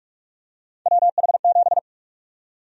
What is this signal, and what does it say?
Morse code, heard as a keyed tone, about 46 wpm, 700 Hz. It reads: WH7
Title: USA Call Sign Prefixes - Repeated 2x - 45wpm